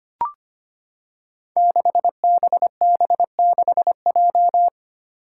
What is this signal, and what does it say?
Morse code, heard as a keyed tone, about 25 wpm, 700 Hz. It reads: 6BB6J
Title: Five Challenging Letter-Number Combinations 25wpm